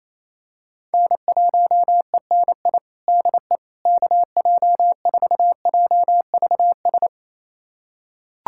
Morse code, heard as a keyed tone, about 28 wpm, 700 Hz. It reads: N1ENI DE KJ4JVS
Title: USA Callsign DE Callsign 28wpm